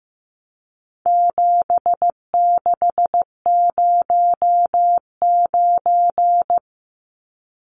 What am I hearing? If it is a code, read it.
Morse code, heard as a keyed tone, about 15 wpm, 700 Hz. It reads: 7609
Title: Numbers - 4-Digit 15wpm